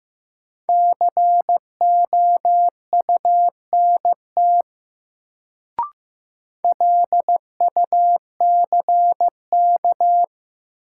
Morse code, heard as a keyed tone, about 15 wpm, 700 Hz. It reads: COUNT LUCK
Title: Sentences from Top 1500 Words - New Word Drill Alt - 15wpm